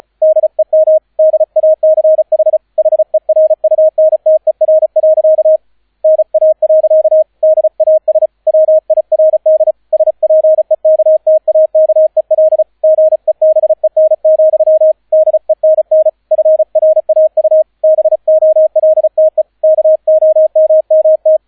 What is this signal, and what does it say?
Morse code, heard as a keyed tone, about 35 wpm, 630 Hz, with radio noise. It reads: DEM DACH HERUNTER. NA. DAS WIRD SPEKTAKEL GEBEN, DENN FRAU BOLTE KOMMT